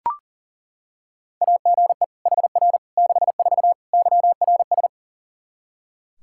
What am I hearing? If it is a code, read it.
Morse code, heard as a keyed tone, about 40 wpm, 700 Hz. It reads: AGE HR 64 YRS
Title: Practice QSO - Exchange 3 - More Info - 40wpm